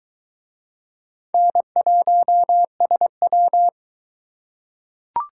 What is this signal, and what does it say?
Morse code, heard as a keyed tone, about 22 wpm, 700 Hz. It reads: N1SW